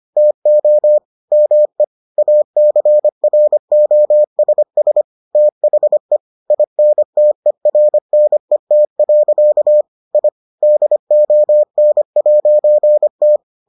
Morse code, heard as a keyed tone, about 25 wpm, 600 Hz. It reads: TO ME ACROSS THE INTERNET. I DON'T